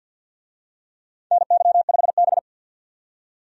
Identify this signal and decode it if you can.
Morse code, heard as a keyed tone, about 53 wpm, 700 Hz. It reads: NY5B
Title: USA Call Signs - 4 Characters - 50wpm